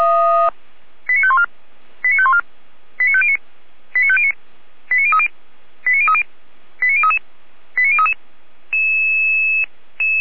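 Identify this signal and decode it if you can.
Five-tone selective call (ZVEI), heard as quick runs of five tones, tones of about 70 ms. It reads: 89514 89514 89509 89509 89030 89030 89033 89033
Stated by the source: ZVEI five-tone selective call 89011 received on SDR.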